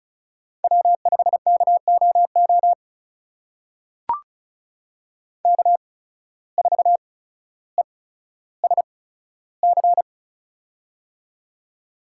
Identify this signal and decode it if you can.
Morse code, heard as a keyed tone, about 36 wpm, spaced as for 15 wpm, 700 Hz. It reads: W5KOO K4ESC
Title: USA Call Signs - 5 Characters - 8x Character Spacing - 35wpm